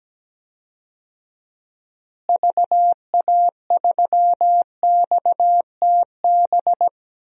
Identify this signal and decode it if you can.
Morse code, heard as a keyed tone, about 17 wpm, 700 Hz. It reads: VA3XTB
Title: DXCC-100 Call Signs - 6 Characters - Repeated 3x - 17wpm